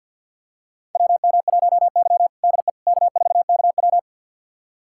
Morse code, heard as a keyed tone, about 52 wpm, 700 Hz. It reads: WM1Y DE K4KW
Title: USA Callsign DE Callsign - 4 Characters - 50wpm